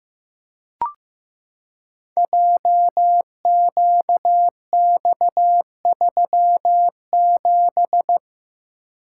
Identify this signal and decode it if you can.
Morse code, heard as a keyed tone, about 15 wpm, 700 Hz. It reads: JQX37
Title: Code Groups - 5 Letter:Numbers - 15wpm